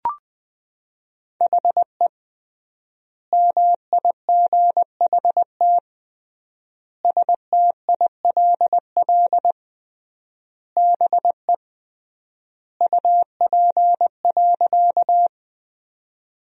Morse code, heard as a keyed tone, about 21 wpm, 700 Hz. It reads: HE MIGHT STILL BE UP.
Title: 3x Word-Spacing - Sentences from Top 300 Words - 20wpm